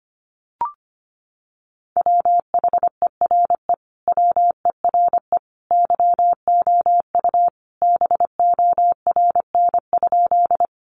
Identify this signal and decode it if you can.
Morse code, heard as a keyed tone, about 25 wpm, 700 Hz. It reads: WHERE WERE YOU BORN?
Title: Sentences from Top 1000 Words 25wpm